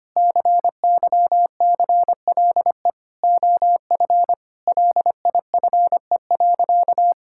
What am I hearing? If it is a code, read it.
Morse code, heard as a keyed tone, about 25 wpm, 700 Hz. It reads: CYCLE OF LIFE.
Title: Sentences from Top 1000 Words - Review Part 6 - 25wpm